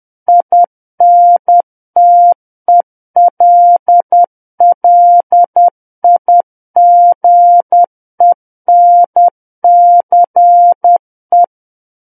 Morse code, heard as a keyed tone, about 10 wpm, 700 Hz. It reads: INTELLIGENCE
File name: Book One / 01-Chapter 1 - 10 WPM